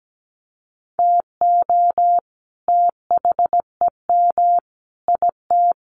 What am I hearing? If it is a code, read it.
Morse code, heard as a keyed tone, about 17 wpm, 700 Hz. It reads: TO THEM IT
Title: Sentences from Top 1400 Words 17wpm